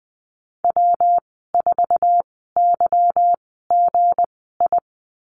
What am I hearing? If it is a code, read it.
Morse code, heard as a keyed tone, about 19 wpm, spaced as for 14 wpm, 700 Hz. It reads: W4YGI
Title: 173 Lesson 28g - Number 9 - Call Signs 20wpm